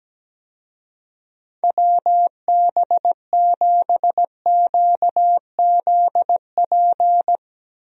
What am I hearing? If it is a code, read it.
Morse code, heard as a keyed tone, about 17 wpm, 700 Hz. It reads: WB7QZP
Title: USA Call Signs - 6 Characters - Repeated 2x - 17wpm